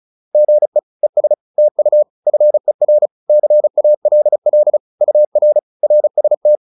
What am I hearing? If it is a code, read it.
Morse code, heard as a keyed tone, about 36 wpm, 600 Hz. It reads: GE ES TU FER CALL UR RST